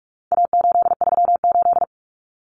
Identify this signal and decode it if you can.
Morse code, heard as a keyed tone, about 44 wpm, 700 Hz. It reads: A838
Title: Common Cut Numbers - 4-Digit 45wpm